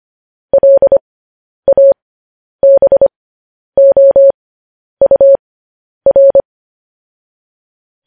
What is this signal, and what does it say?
Morse code, heard as a keyed tone, about 25 wpm, spaced as for 10 wpm, 550 Hz. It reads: LABOUR
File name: podcast / News headlines-1x-25x10